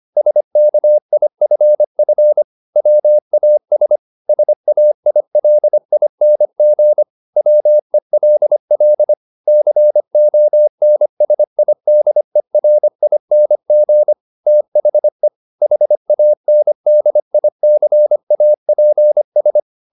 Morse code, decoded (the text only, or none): SKIFF WAS SAILING WELL CONSIDERING THE HANDICAPS